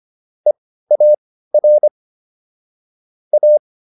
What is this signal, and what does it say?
Morse code, heard as a keyed tone, about 25 wpm, spaced as for 15 wpm, 600 Hz. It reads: EAR A
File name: SS203 15